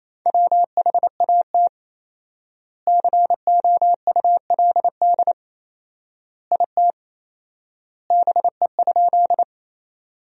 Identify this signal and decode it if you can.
Morse code, heard as a keyed tone, about 28 wpm, 700 Hz. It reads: WHAT COULD IT BE?
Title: 4x Word-Spacing - Sentences from Top 100 Words - 28wpm